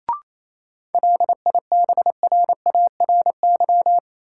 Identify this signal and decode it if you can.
Morse code, heard as a keyed tone, about 28 wpm, 700 Hz. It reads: LIBRARY
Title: Sentences from Top 600 Words - New Word Drill - 28wpm